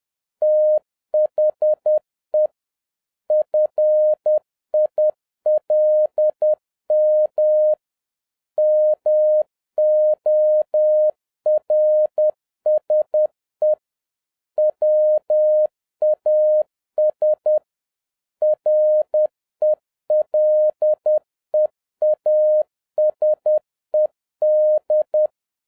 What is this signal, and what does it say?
Morse code, heard as a keyed tone, about 10 wpm, 610 Hz. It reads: THE FILM MORSE WAS RELEASED